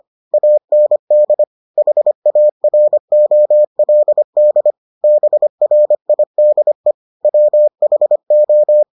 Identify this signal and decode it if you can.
Morse code, heard as a keyed tone, about 25 wpm, 600 Hz. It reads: AND HAROLD BRIDE WHO